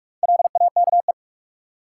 Morse code, heard as a keyed tone, about 43 wpm, 700 Hz. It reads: LAKE
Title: Four Letter Words - 45wpm